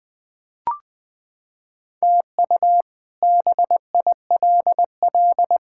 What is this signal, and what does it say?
Morse code, heard as a keyed tone, about 20 wpm, 700 Hz. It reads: TU BILL
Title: K1USN SST QSO Practice - 20wpm